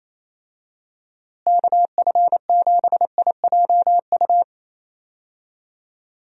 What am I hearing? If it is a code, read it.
Morse code, heard as a keyed tone, about 28 wpm, 700 Hz. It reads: KF7IJU
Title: USA Call Signs - 6 Characters - Repeated 3x - 28wpm